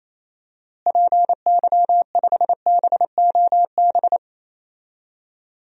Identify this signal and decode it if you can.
Morse code, heard as a keyed tone, about 27 wpm, 700 Hz. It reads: PY5BOB
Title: DXCC-100 Call Signs - 6 Characters - Repeated 2x - 28wpm